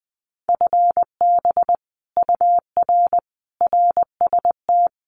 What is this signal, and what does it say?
Morse code, heard as a keyed tone, about 20 wpm, 700 Hz. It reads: FB UR RST
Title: Common QSO Phrases Encore 20wpm